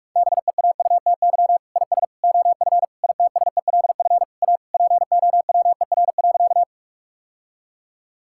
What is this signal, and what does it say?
Morse code, heard as a keyed tone, about 46 wpm, 700 Hz. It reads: BEAUTY IS OF ITSELF A POWER.
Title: Sentences from Top 1000 Words - Review Part 5 - 45wpm